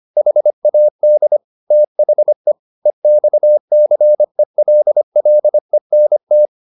Morse code, heard as a keyed tone, about 25 wpm, 600 Hz. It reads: HAD THE EXCELLENT
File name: the old man and the sea part 1 25wpm